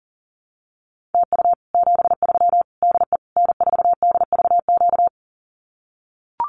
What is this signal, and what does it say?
Morse code, heard as a keyed tone, about 40 wpm, 700 Hz. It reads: TU 73 DE N4DVQ
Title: Common QSO Phrases Encore 40wpm